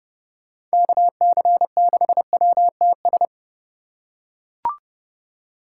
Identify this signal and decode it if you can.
Morse code, heard as a keyed tone, about 29 wpm, 700 Hz. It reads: KC6WTS